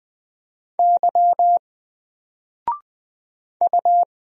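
Morse code, heard as a keyed tone, about 20 wpm, 700 Hz. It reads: Y U
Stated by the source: Single letter and number Morse drill at 20 wpm, from N4LCD.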